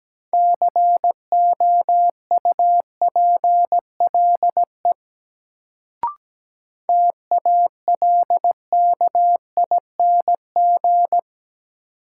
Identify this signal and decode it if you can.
Morse code, heard as a keyed tone, about 17 wpm, 700 Hz. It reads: COUPLE TALKING
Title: Sets of 2 Words - New Word Drill - 17wpm